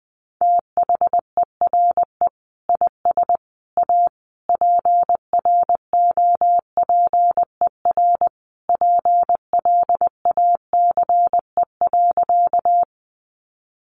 Morse code, heard as a keyed tone, about 20 wpm, 700 Hz. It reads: THERE IS A PROPER PLACE.
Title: Sentences from Top 800 Words 20wpm